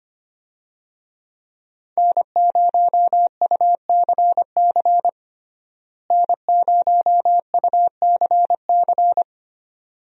Morse code, heard as a keyed tone, about 25 wpm, 700 Hz. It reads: N0UCC N0UCC
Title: USA Call Signs - 5 Characters - Repeated 2x - 25wpm